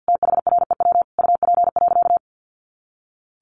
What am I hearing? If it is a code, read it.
Morse code, heard as a keyed tone, about 49 wpm, 700 Hz. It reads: THREW UP.
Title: A Sound Of Thunder 50wpm